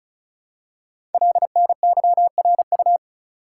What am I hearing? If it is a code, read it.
Morse code, heard as a keyed tone, about 36 wpm, 700 Hz. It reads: LNYRU
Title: Code Groups - 5 Letter:Numbers - 35wpm